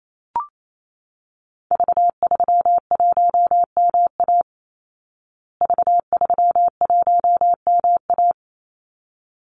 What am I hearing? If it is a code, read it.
Morse code, heard as a keyed tone, about 28 wpm, 700 Hz. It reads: V31MA V31MA
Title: Call Signs Repeated 3x - 28wpm